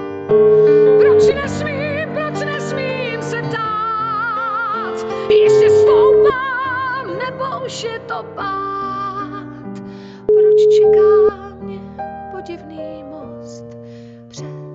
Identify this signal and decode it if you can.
Ringback tone (Europe, 425 Hz).